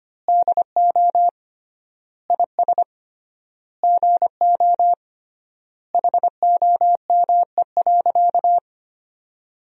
Morse code, heard as a keyed tone, about 25 wpm, 700 Hz. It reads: DO IS GO HOME.
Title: 3x Word-Spacing - Sentences from Top 200 Words - 25wpm